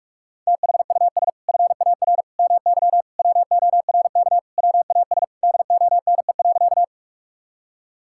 Morse code, heard as a keyed tone, about 41 wpm, 700 Hz. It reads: THUS FAR MY WORK WAS DONE.
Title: Sentences from Top 1000 Words - Review Part 5 - 45wpm